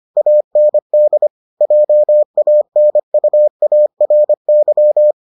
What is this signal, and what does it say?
Morse code, heard as a keyed tone, about 25 wpm, 600 Hz. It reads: AND JANUARY